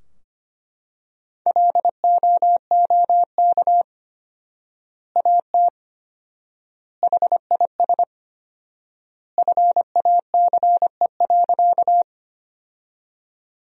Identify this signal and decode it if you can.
Morse code, heard as a keyed tone, about 25 wpm, 700 Hz. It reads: LOOK AT HIS FACE.
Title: 4x Word-Spacing - Sentences from Top 300 Words - 25wpm